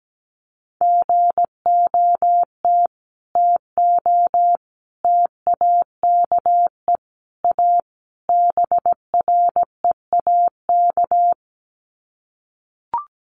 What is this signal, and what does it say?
Morse code, heard as a keyed tone, about 17 wpm, 700 Hz. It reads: GOT TO TAKE A BREAK
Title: Sets of 5 Words Encore 17wpm